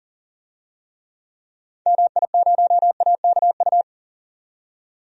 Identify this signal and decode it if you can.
Morse code, heard as a keyed tone, about 40 wpm, 700 Hz. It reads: MI0AKU
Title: DXCC-100 Call Signs - 6 Characters - 40wpm